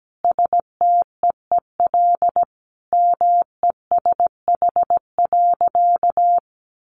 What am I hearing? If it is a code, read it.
Morse code, heard as a keyed tone, about 17 wpm, 700 Hz. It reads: STEEL MESH.
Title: A Sound Of Thunder 17wpm